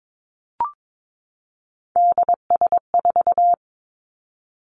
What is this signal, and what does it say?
Morse code, heard as a keyed tone, about 22 wpm, 700 Hz. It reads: DS4
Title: Three Challenging Letter-Number Combinations 22wpm